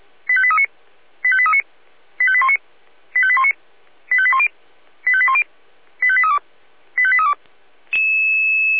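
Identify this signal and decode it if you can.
Five-tone selective call (ZVEI), heard as quick runs of five tones, tones of about 70 ms. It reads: 85839 85839 85819 85819 85810 85810 85832 85832
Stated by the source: ZVEI five-tone selective call received on an SDR.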